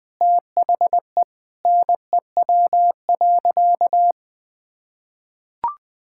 Morse code, heard as a keyed tone, about 21 wpm, 700 Hz. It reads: THE NEW.
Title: Sentences with 9-Word Phrases - Spoken Beginning - 20wpm